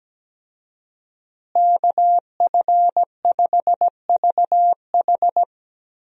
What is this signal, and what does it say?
Morse code, heard as a keyed tone, about 17 wpm, 700 Hz. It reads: KF5VH
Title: USA Call Signs - 5 Characters - Repeated 3x - 17wpm